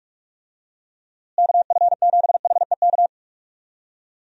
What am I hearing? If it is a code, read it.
Morse code, heard as a keyed tone, about 46 wpm, 700 Hz. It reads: KF7HEK
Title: USA Call Signs - 6 Characters - Repeated 3x - 45wpm